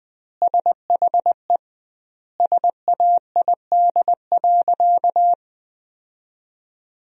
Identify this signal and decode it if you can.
Morse code, heard as a keyed tone, about 20 wpm, 700 Hz. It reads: SHE SAID.
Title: 2x Word-Spacing - Sentences from Top 400 Words - 20wpm